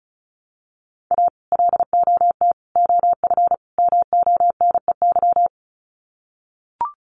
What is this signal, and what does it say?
Morse code, heard as a keyed tone, about 34 wpm, 700 Hz. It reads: A LOT OF MONEY
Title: Sets of 4 Words 35wpm